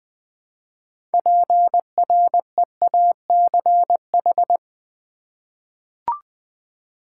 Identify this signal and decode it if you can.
Morse code, heard as a keyed tone, about 20 wpm, 700 Hz. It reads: PREACH
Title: 6 Letter Words - First 3 Letters Given - 20wpm